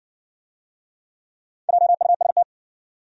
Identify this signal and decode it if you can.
Morse code, heard as a keyed tone, about 60 wpm, 700 Hz. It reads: JUST